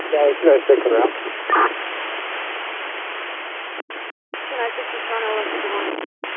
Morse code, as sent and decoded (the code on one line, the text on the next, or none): none